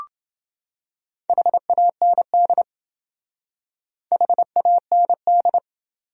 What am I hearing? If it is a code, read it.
Morse code, heard as a keyed tone, about 28 wpm, 700 Hz. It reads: HAND HAND